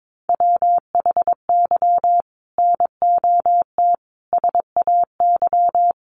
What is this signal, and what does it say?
Morse code, heard as a keyed tone, about 22 wpm, 700 Hz. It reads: WHY NOT SAY
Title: Sentences from Top 600 Words 22wpm